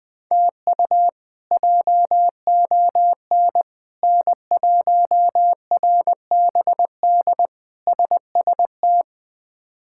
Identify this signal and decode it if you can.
Morse code, heard as a keyed tone, about 21 wpm, 700 Hz. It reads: TU JON N1RBD SST
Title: K1USN SST QSO Practice - 20wpm